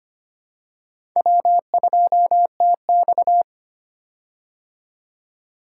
Morse code, heard as a keyed tone, about 25 wpm, 700 Hz. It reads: W2TX
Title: K1USN SST QSO Practice - 25wpm